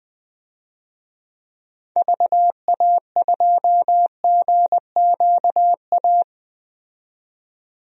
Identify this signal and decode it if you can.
Morse code, heard as a keyed tone, about 20 wpm, 700 Hz. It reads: VA2GQA
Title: DXCC-100 Call Signs - 6 Characters - 20wpm